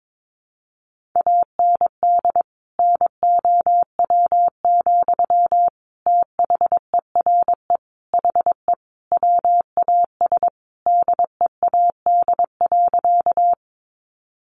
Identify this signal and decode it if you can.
Morse code, heard as a keyed tone, about 22 wpm, 700 Hz. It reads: AND NOW, THERE HE WAS DEAD.
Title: Sentences from Top 1500 Words 22wpm